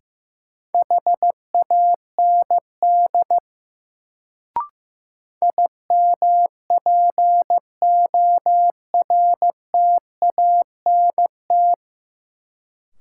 Morse code, heard as a keyed tone, about 15 wpm, 700 Hz. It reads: HAND IMPORTANT
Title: Sets of 3 Words - New Word Drill - 15wpm